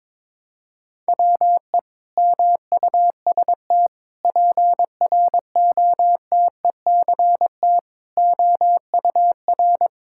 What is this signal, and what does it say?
Morse code, heard as a keyed tone, about 21 wpm, 700 Hz. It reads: WE MUST PROTECT OUR
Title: Sentences from Top 1000 Words - Review Part 5 - 22wpm